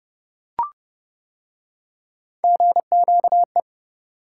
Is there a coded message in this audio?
Yes: Morse code from 2.4 to 3.6 s, heard as a keyed tone, about 29 wpm, 700 Hz. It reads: GQE